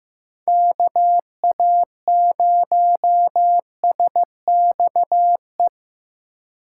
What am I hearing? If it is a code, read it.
Morse code, heard as a keyed tone, about 15 wpm, 700 Hz. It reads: KA0SXE
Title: USA Call Signs - 6 Characters - 4x Character Spacing - 15wpm